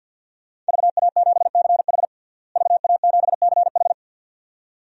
Morse code, heard as a keyed tone, about 52 wpm, 700 Hz. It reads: VA7CH VA7CH
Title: DXCC-100 Call Signs - 5 Characters - Repeated 2x - 50wpm